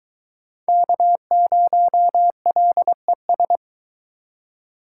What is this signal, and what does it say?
Morse code, heard as a keyed tone, about 22 wpm, 700 Hz. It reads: K0LES